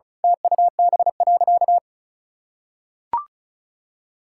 Morse code, heard as a keyed tone, about 36 wpm, 700 Hz. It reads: TUB.
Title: Pangram Challenge v2 - 35wpm